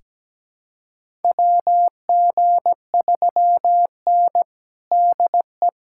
Morse code, heard as a keyed tone, about 17 wpm, 700 Hz. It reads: WG3N DE
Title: USA Callsign DE Callsign - 4 Characters - 17wpm